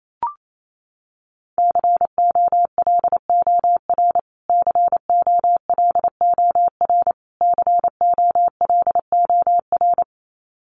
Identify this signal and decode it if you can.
Morse code, heard as a keyed tone, about 28 wpm, 700 Hz. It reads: COLOR COLOR COLOR
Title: Top 500 Words Repeated 28wpm